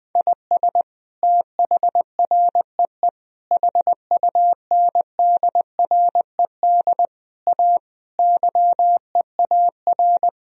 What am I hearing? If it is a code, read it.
Morse code, heard as a keyed tone, about 20 wpm, 700 Hz. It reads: IS THREE HUNDRED A YEAR
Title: Sentences from Top 1000 Words - Review Part 2 - 20wpm